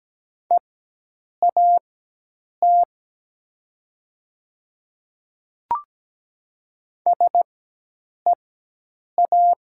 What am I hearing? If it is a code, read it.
Morse code, heard as a keyed tone, about 17 wpm, spaced as for 8 wpm, 700 Hz. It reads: EAT SEA
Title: 4x Character Spacing - Top 300 Words - 17wpm